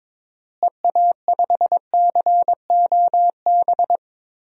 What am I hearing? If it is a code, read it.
Morse code, heard as a keyed tone, about 22 wpm, 700 Hz. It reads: EA5COB
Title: DXCC-100 Call Signs - 6 Characters - Repeated 3x - 22wpm